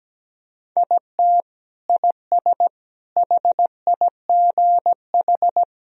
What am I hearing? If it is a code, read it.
Morse code, heard as a keyed tone, about 17 wpm, 700 Hz. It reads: IT IS HIGH
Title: Sentences from Top 1000 Words - Review Part 1 - 17wpm